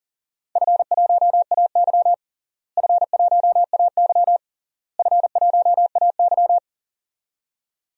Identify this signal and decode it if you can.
Morse code, heard as a keyed tone, about 40 wpm, 700 Hz. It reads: F1AY F1AY F1AY
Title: DXCC-100 Call Signs - 4 Characters - Repeated 3x - 40wpm